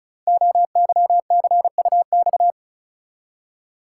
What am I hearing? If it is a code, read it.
Morse code, heard as a keyed tone, about 33 wpm, 700 Hz. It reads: OYCUX
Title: Code Groups - 5 Letter:Numbers - 35wpm